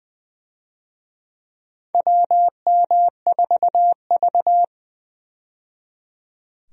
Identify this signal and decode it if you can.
Morse code, heard as a keyed tone, about 21 wpm, 700 Hz. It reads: WM4V
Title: USA Call Signs - 4 Characters - 20wpm